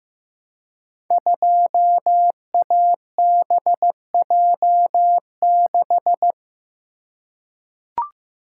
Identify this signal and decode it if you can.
Morse code, heard as a keyed tone, about 15 wpm, 700 Hz. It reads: 2ABJ6
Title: Code Groups - 5 Letter:Numbers - 15wpm